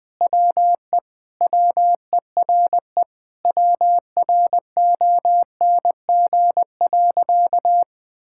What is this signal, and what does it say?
Morse code, heard as a keyed tone, about 20 wpm, 700 Hz. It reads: WE WERE WRONG.
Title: Sentences with 5-Word Phrases - 20wpm